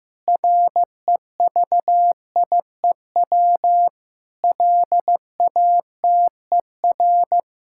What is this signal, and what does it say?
Morse code, heard as a keyed tone, about 15 wpm, 700 Hz. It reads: REVIEW LATER
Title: Sentences from Top 1000 Words - Review Part 6 - 15wpm